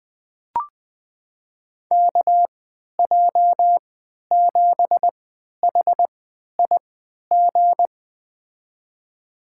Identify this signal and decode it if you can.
Morse code, heard as a keyed tone, about 21 wpm, spaced as for 11 wpm, 700 Hz. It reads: KJ7HIG
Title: USA Call Signs - 3x Character Spacing - 20wpm